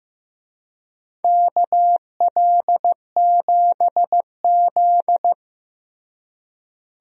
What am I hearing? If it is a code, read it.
Morse code, heard as a keyed tone, about 15 wpm, 700 Hz. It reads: KL7Z
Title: DXCC-100 Call Signs - 4 Characters - Repeated 2x - 15wpm